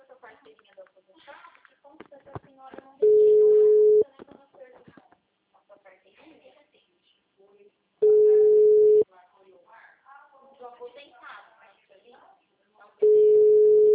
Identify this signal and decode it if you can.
Ringback tone (Europe, 425 Hz), with speech.